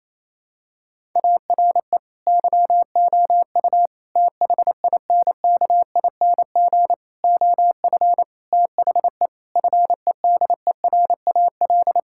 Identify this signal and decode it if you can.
Morse code, heard as a keyed tone, about 28 wpm, 700 Hz. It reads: ARE YOU THINKING OF THE FEDERAL
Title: Sentences from Top 1000 Words - Review Part 6 - 28wpm